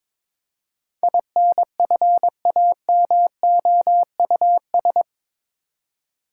Morse code, heard as a keyed tone, about 22 wpm, 700 Hz. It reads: INFAMOUS
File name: practice / 22wpm / Eight Letter Words - 22wpm